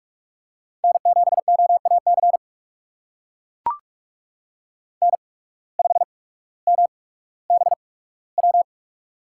Morse code, heard as a keyed tone, about 44 wpm, spaced as for 19 wpm, 690 Hz. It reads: N7OAC N5MBW
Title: USA Call Signs - 5 Characters - 8x Character Spacing - 45wpm